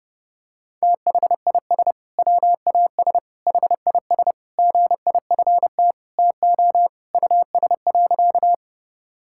Morse code, heard as a keyed tone, about 32 wpm, 700 Hz. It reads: THIS WAS HIS GIFT TO US.
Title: Sentences from Top 1000 Words - Review Part 6 - 30wpm